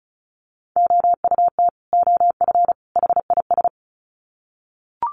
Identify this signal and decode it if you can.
Morse code, heard as a keyed tone, about 34 wpm, 700 Hz. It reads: OUT OF HIS
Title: Sets of 3 Words 35wpm